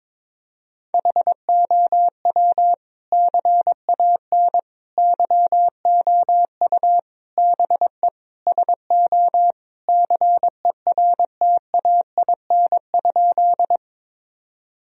Morse code, heard as a keyed tone, about 22 wpm, 700 Hz. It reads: HOW CAN YOU BE SO CERTAIN?
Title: Sentences from Top 1000 Words - Review Part 2 - 22wpm